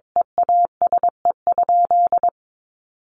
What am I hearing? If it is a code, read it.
Morse code, heard as a keyed tone, about 22 wpm, 700 Hz. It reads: EASE?
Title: Sentences from Top 1500 Words 22wpm